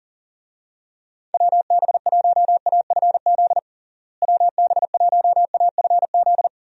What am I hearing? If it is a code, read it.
Morse code, heard as a keyed tone, about 40 wpm, 700 Hz. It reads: WB1AFZ WB1AFZ
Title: USA Call Signs - 6 Characters - Repeated 3x - 40wpm